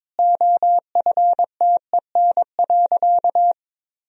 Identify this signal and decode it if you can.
Morse code, heard as a keyed tone, about 22 wpm, 700 Hz. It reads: OFTEN.
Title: Sentences from Top 1000 Words - Review Part 1 - 22wpm